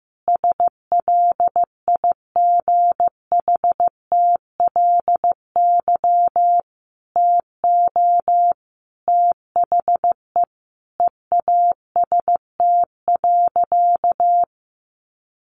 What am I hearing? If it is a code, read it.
Morse code, heard as a keyed tone, about 15 wpm, 700 Hz. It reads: SLIGHTLY TO THE EAST.
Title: Sentences from Top 1300 Words 15wpm